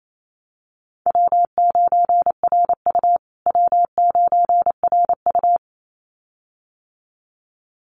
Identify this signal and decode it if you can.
Morse code, heard as a keyed tone, about 28 wpm, 700 Hz. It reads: W9RU W9RU
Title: USA Call Signs Repeated 2x 28wpm